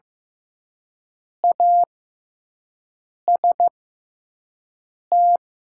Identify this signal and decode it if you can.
Morse code, heard as a keyed tone, about 15 wpm, spaced as for 5 wpm, 700 Hz. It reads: AST
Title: 6x Character Spacing - Top 500 Words - 15wpm